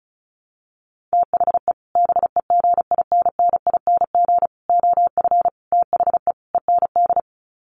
Morse code, heard as a keyed tone, about 34 wpm, 700 Hz. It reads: THE BEGINNING OF THE END
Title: Sets of 5 Words Encore2 35wpm